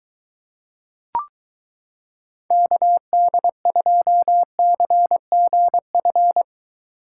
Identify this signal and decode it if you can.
Morse code, heard as a keyed tone, about 23 wpm, 700 Hz. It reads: KD2CGF